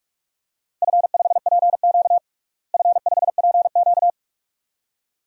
Morse code, heard as a keyed tone, about 46 wpm, 700 Hz. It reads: F5PQ F5PQ
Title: DXCC-100 Call Signs - 4 Characters - Repeated 2x - 45wpm